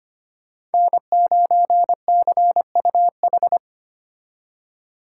Morse code, heard as a keyed tone, about 31 wpm, 700 Hz. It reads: N9CUH
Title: Speed Racing - USA 1x3 Call Signs 25wpm